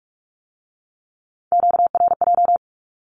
Morse code, heard as a keyed tone, about 46 wpm, 700 Hz. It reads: QRJ
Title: Q-Codes with Spoken Meaning - 45wpm